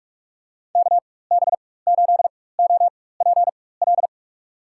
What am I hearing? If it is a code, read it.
Morse code, heard as a keyed tone, about 43 wpm, spaced as for 21 wpm, 700 Hz. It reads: KB8OPL
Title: USA Call Signs - 6 Characters - 4x Character Spacing - 45wpm